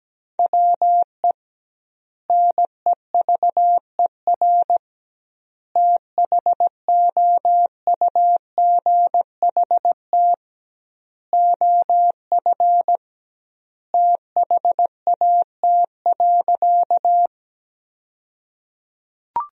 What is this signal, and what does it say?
Morse code, heard as a keyed tone, about 17 wpm, 700 Hz. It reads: WE NEVER THOUGHT OF THAT.
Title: 2x Word-Spacing - Sentences from Top 300 Words - 17wpm